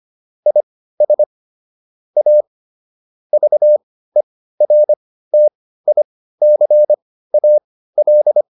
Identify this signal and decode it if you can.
Morse code, heard as a keyed tone, about 25 wpm, spaced as for 15 wpm, 600 Hz. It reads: IS A VERTICAL